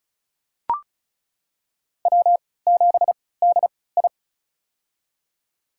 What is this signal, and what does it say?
Morse code, heard as a keyed tone, about 36 wpm, spaced as for 20 wpm, 700 Hz. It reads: W7DI
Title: USA Call Signs - 3x Character Spacing - 35wpm